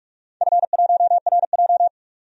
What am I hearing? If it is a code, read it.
Morse code, heard as a keyed tone, about 41 wpm, 700 Hz. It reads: F1RJ
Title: DXCC-100 Call Signs - 4 Characters - 45wpm